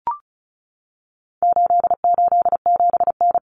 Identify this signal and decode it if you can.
Morse code, heard as a keyed tone, about 34 wpm, 700 Hz. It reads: 887N
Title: Common Cut Numbers - 4-Digit 35wpm